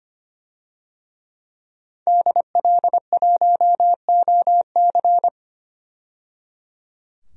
Morse code, heard as a keyed tone, about 25 wpm, 700 Hz. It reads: DL1OC
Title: DXCC-100 Call Signs - 5 Characters - 25wpm